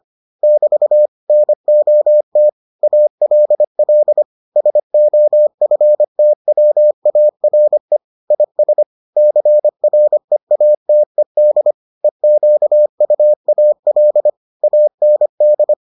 Morse code, heard as a keyed tone, about 25 wpm, 600 Hz. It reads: <BT> NOT ALL SOFTWARE IS CREATED EQUAL AND